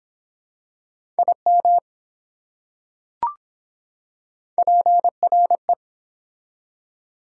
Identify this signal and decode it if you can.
Morse code, heard as a keyed tone, about 23 wpm, 700 Hz. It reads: IM PRE